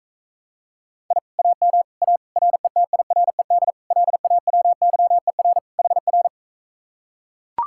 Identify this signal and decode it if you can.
Morse code, heard as a keyed tone, about 44 wpm, 700 Hz. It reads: I AM A RETIRED LAWYER HR